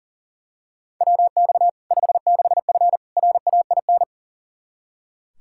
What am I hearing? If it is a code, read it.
Morse code, heard as a keyed tone, about 40 wpm, 700 Hz. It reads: WX 56F RAIN
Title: Practice QSO - Exchange 2 - Info - 40wpm